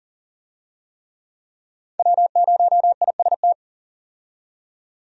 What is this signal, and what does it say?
Morse code, heard as a keyed tone, about 40 wpm, 700 Hz. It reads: W0IST
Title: USA Call Signs - 5 Characters - 40wpm